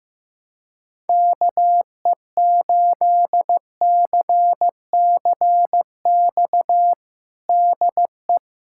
Morse code, heard as a keyed tone, about 15 wpm, 700 Hz. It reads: KE8CCX DE
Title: USA Callsign DE Callsign - 6 Characters - 15wpm